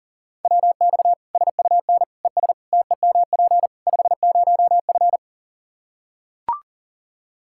Morse code, heard as a keyed tone, about 40 wpm, 700 Hz. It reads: WX SUN ES TEMP 50F
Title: Practice QSO - Exchange 2 - Info - 40wpm